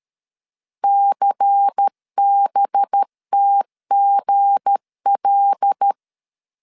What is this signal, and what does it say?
Morse code, heard as a keyed tone, about 13 wpm, 800 Hz. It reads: CBTGL
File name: LEZIONI / 7 GF9Z / 7 GF9 12wpm 2319